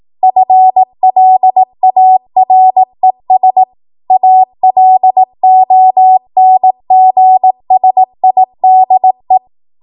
Morse code, heard as a keyed tone, about 18 wpm, 750 Hz. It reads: FLARES ALONGSIDE
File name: Morse / 18WPM